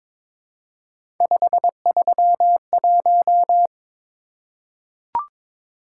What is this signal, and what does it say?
Morse code, heard as a keyed tone, about 22 wpm, 700 Hz. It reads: 531
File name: practice / 22wpm / Three Letter-Number Combinations - 22wpm